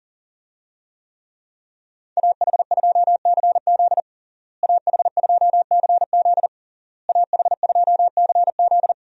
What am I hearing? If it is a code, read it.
Morse code, heard as a keyed tone, about 37 wpm, 700 Hz. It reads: AH2CZ AH2CZ AH2CZ
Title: USA Call Signs - 5 Characters - Repeated 3x - 40wpm